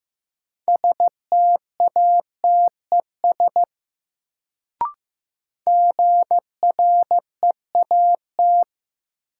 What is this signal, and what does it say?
Morse code, heard as a keyed tone, about 15 wpm, 700 Hz. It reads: STATES GREAT
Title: Sets of 2 Words - New Word Drill - 15wpm